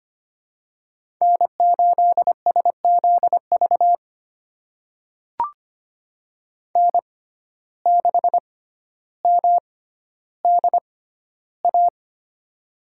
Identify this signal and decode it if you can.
Morse code, heard as a keyed tone, about 25 wpm, spaced as for 13 wpm, 700 Hz. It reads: N8SZV N6MDA
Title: USA Call Signs - 6x Character Spacing - 25wpm